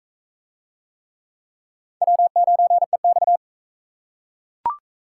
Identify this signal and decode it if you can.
Morse code, heard as a keyed tone, about 40 wpm, 700 Hz. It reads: W9EX